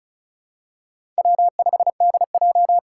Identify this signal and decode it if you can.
Morse code, heard as a keyed tone, about 36 wpm, 700 Hz. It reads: W5DJ
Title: Four Challenging Letter-Number Combinations 35wpm